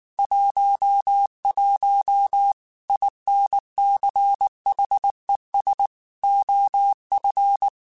Morse code, heard as a keyed tone, about 19 wpm, 800 Hz. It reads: 11 INCHES OF
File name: QSOs / QSOs-mp3s / WA7QOX 1@18